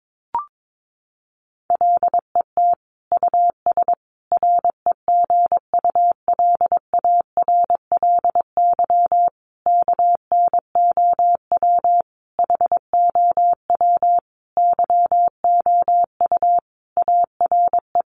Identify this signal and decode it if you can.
Morse code, heard as a keyed tone, about 22 wpm, 700 Hz. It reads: LET US REGULARLY KNOW HOW YOU ARE
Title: Sentences from Top 1400 Words 22wpm